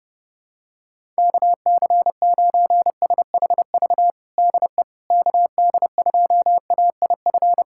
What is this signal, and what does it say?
Morse code, heard as a keyed tone, about 30 wpm, 700 Hz. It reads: KC9SHV DE KD2AIF
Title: USA Callsign DE Callsign - 6 Characters - 30wpm